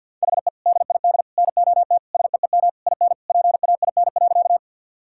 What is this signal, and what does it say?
Morse code, heard as a keyed tone, about 45 wpm, 700 Hz. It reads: HE DID NOT SEEM IN PAIN.
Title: Sentences from Top 1600 Words 50wpm